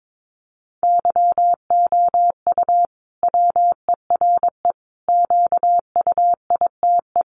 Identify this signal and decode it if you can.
Morse code, heard as a keyed tone, about 22 wpm, 700 Hz. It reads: YOU WERE QUITE
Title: Sentences from Top 1200 Words 22wpm